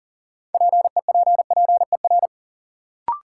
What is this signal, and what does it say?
Morse code, heard as a keyed tone, about 40 wpm, 700 Hz. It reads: PEPPER
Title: Sets of 5 Words - New Word Drill - 40wpm